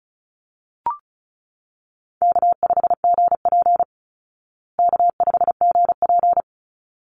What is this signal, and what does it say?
Morse code, heard as a keyed tone, about 34 wpm, 700 Hz. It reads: K5GP K5GP
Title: Easy Call Signs Repeated 2x - 35wpm